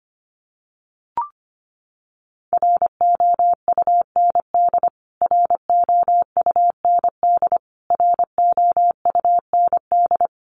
Morse code, heard as a keyed tone, about 25 wpm, 700 Hz. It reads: ROUND ROUND ROUND
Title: Top 500 Words Repeated 25wpm